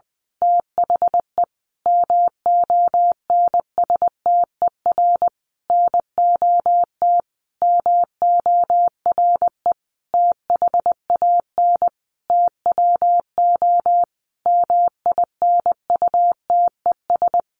Morse code, heard as a keyed tone, about 20 wpm, 700 Hz. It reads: THE MONSTER NOT MORE THAN TWO MINUTES
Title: A Sound Of Thunder 20wpm